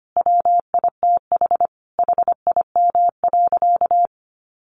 Morse code, heard as a keyed tone, about 25 wpm, 700 Hz. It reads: WITH HIM.
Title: Sentences from Top 1500 Words 25wpm